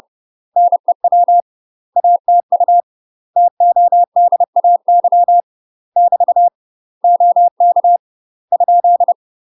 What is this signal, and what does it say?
Morse code, heard as a keyed tone, about 30 wpm, 700 Hz. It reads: NEW ATU TODAY <BT> OK ?
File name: UK MASTER QSOs TRAINING/30WPM